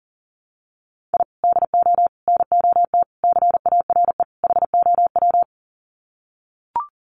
Morse code, heard as a keyed tone, about 37 wpm, 700 Hz. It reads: I DO NOT CARE HOW
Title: Sets of 5 Words Encore2 40wpm